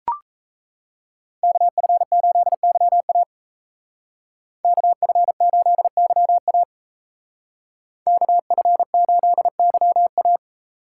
Morse code, heard as a keyed tone, about 33 wpm, 700 Hz. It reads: KF8YA KF8YA KF8YA
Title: Speed Racing - USA Call Signs 28wpm